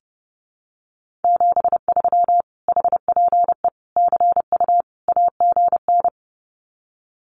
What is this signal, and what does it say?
Morse code, heard as a keyed tone, about 30 wpm, 700 Hz. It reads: 73 HPE CU AGN
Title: Common QSO Phrases Encore 30wpm